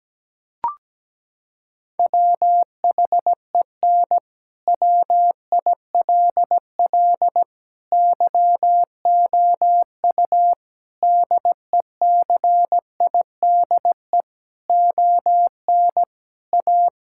Morse code, heard as a keyed tone, about 17 wpm, 700 Hz. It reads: WHEN WILL YOU DECIDE ON A
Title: Sentences from Top 1000 Words - Review Part 6 - 17wpm